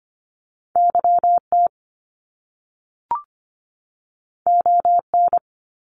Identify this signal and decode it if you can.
Morse code, heard as a keyed tone, about 25 wpm, 700 Hz. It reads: YT ON
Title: Top Two Letter Combinations in Words 25wpm